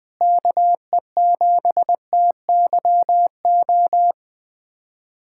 Morse code, heard as a keyed tone, about 21 wpm, 700 Hz. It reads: KE7TYO
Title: USA Callsign DE Callsign 20wpm